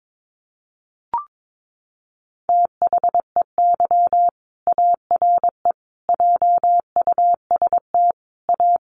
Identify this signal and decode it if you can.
Morse code, heard as a keyed tone, about 22 wpm, 700 Hz. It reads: THEY ARE JUST A
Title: Sentences from Top 1100 Words 22wpm